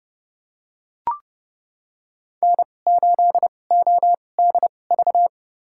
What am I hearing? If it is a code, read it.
Morse code, heard as a keyed tone, about 32 wpm, spaced as for 22 wpm, 690 Hz. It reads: N8ODV
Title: USA Call Signs - 5 Characters - 2x Character Spacing - 30wpm